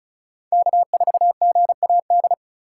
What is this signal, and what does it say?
Morse code, heard as a keyed tone, about 36 wpm, 700 Hz. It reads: K4GAD
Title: USA Call Signs - 2x Character Spacing - 35wpm